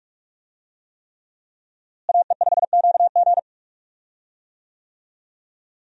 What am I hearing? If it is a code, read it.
Morse code, heard as a keyed tone, about 46 wpm, 700 Hz. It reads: AE5QG
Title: USA Call Signs - 5 Characters - Repeated 3x - 45wpm